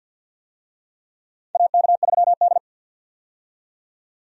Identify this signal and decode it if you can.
Morse code, heard as a keyed tone, about 47 wpm, 700 Hz. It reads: AK3D
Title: USA Call Signs - 4 Characters - 50wpm